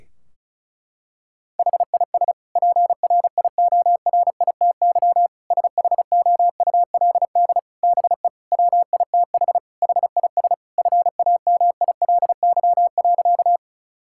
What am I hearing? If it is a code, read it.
Morse code, heard as a keyed tone, about 36 wpm, 700 Hz. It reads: HIS PRIORITY SHOULD BE WITH HIS FAMILY.
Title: Sentences from Top 1600 Words 35wpm